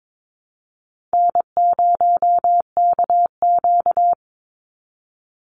Morse code, heard as a keyed tone, about 22 wpm, 700 Hz. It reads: N0KQ
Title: Easy Call Signs Repeated 3x - 22wpm